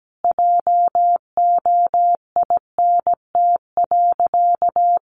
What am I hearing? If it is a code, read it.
Morse code, heard as a keyed tone, about 17 wpm, 700 Hz. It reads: JOINT.
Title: Sentences from Top 1300 Words 17wpm